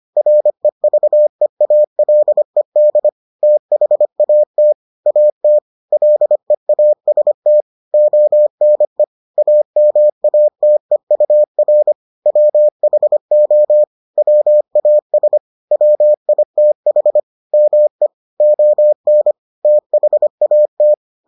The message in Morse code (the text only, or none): REVEALED THAT AT LEAST ONE AMATEUR WHO WAS WITH ME ON THAT